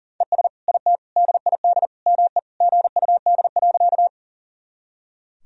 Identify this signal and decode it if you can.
Morse code, heard as a keyed tone, about 40 wpm, 700 Hz. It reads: ES IT DID ME GUD.
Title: Sentences from Top 200 Words and Abbreviations 40wpm